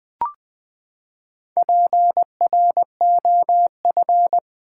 Morse code, heard as a keyed tone, about 20 wpm, 700 Hz. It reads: PROF
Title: Build A Word - 4 Letters At A Time - 12-Letter Words - 20wpm